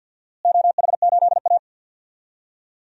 Morse code, heard as a keyed tone, about 52 wpm, 700 Hz. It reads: OH8A
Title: DXCC-100 Call Signs - 4 Characters - Repeated 2x - 50wpm